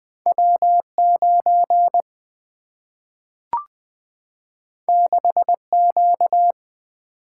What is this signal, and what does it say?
Morse code, heard as a keyed tone, about 20 wpm, 700 Hz. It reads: W9 6Q
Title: Single-Letter Single-Number 20wpm